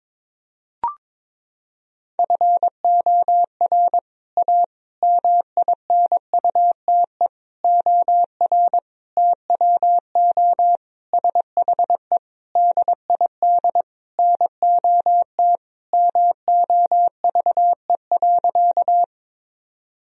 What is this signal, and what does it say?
Morse code, heard as a keyed tone, about 22 wpm, 700 Hz. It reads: FOR A MINUTE OR TWO SHE DID NOT MOVE.
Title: Sentences from Top 1000 Words - Review Part 1 - 22wpm